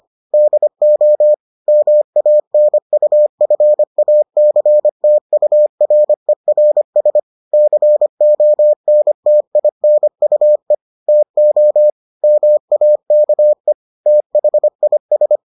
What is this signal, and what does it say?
Morse code, heard as a keyed tone, about 25 wpm, 600 Hz. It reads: DO MANUFACTURERS CONTINUE TO MAKE THIS